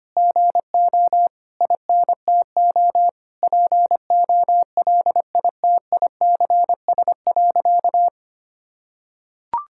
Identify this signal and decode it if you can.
Morse code, heard as a keyed tone, about 25 wpm, 700 Hz. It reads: GO INTO POLITICS.
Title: Sentences from Top 1000 Words - Review Part 6 - 25wpm